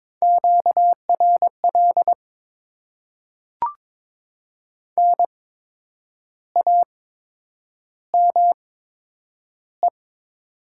Morse code, heard as a keyed tone, about 22 wpm, spaced as for 8 wpm, 700 Hz. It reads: QRL NAME
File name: Top 20 QSO Elements - 8x Character Spacing - 22wpm